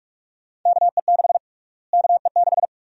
Morse code, heard as a keyed tone, about 46 wpm, 700 Hz. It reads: KE6 KE6
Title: USA Call Sign Prefixes - Repeated 2x - 45wpm